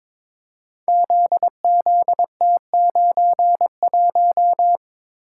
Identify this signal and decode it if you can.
Morse code, heard as a keyed tone, about 22 wpm, 690 Hz. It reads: ZZT91
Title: Code Groups - 5 Letter:Numbers - 22wpm